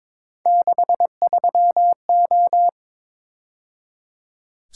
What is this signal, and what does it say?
Morse code, heard as a keyed tone, about 26 wpm, 700 Hz. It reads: 63O
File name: Speed-Racing - Three Letter-Number Combinations - 22wpm